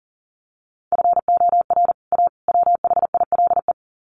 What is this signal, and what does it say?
Morse code, heard as a keyed tone, about 40 wpm, 700 Hz. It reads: FOR A WHILE
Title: Sets of 3 Words 40wpm